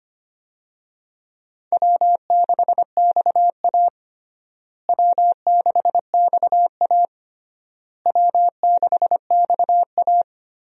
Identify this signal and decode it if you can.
Morse code, heard as a keyed tone, about 25 wpm, 700 Hz. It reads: W6XA W6XA W6XA
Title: USA Call Signs - 4 Characters - Repeated 3x - 25wpm